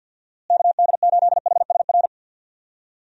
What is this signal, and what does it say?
Morse code, heard as a keyed tone, about 48 wpm, 700 Hz. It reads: KD8HSR
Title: USA Call Signs - 6 Characters - Repeated 2x - 50wpm